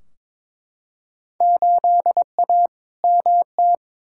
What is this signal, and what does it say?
Morse code, heard as a keyed tone, about 22 wpm, 700 Hz. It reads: 8A MT
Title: Field Day Exchanges Encore - 22wpm